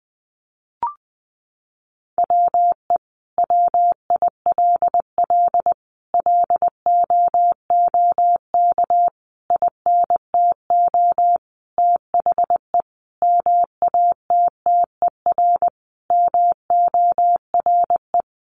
Morse code, heard as a keyed tone, about 20 wpm, 700 Hz. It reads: WE WILL LOOK INTO THE MATTER MORE
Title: Sentences from Top 1400 Words 20wpm